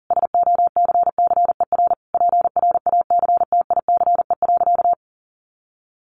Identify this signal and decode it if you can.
Morse code, heard as a keyed tone, about 40 wpm, 700 Hz. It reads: SOCCER PRACTICE.
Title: Easy Phrases 40wpm